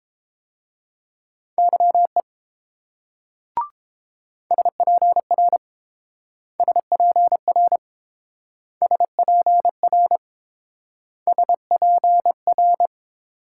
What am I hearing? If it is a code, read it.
Morse code, heard as a keyed tone, about 30 wpm, 700 Hz. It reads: YE SPR SPR SPR SPR